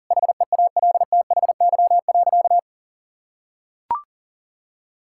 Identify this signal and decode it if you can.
Morse code, heard as a keyed tone, about 37 wpm, 700 Hz. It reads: HEALTHY.
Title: Sentences from Top 1000 Words - Review Part 5 - 40wpm